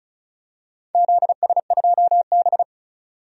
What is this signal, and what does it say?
Morse code, heard as a keyed tone, about 34 wpm, 700 Hz. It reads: ZS2B
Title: DXCC-100 Call Signs - 4 Characters - 35wpm